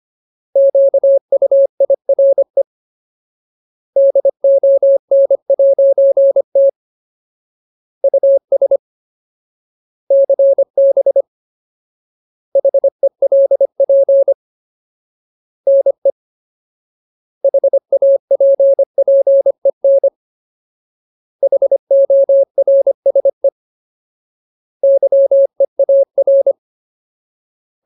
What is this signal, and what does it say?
Morse code, heard as a keyed tone, about 25 wpm, 550 Hz. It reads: QUIRE DON'T US CB HELP NE HAPPEN HORSE YEAR
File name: Ultimate Word List / 25WPM Ultimate